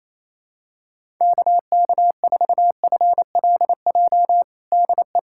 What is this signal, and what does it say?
Morse code, heard as a keyed tone, about 28 wpm, 700 Hz. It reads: KK4FLJ DE
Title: USA Callsign DE Callsign - 6 Characters - 28wpm